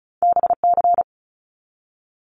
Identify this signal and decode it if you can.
Morse code, heard as a keyed tone, about 36 wpm, 700 Hz. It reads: BC
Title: Canadian Provinces and Territories 35wpm